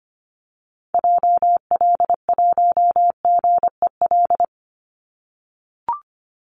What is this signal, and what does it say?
Morse code, heard as a keyed tone, about 25 wpm, 700 Hz. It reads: JL1GEL
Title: Call Signs - 25wpm